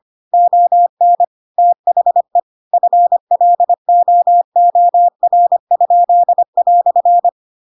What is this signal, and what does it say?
Morse code, heard as a keyed tone, about 25 wpm, 700 Hz. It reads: ON THE FLOOR?"
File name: Pinocchio / Pinocchio ch2 25 25 ik1hge